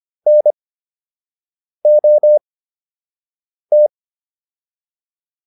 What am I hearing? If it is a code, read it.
Morse code, heard as a keyed tone, about 25 wpm, spaced as for 6 wpm, 600 Hz. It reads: NOT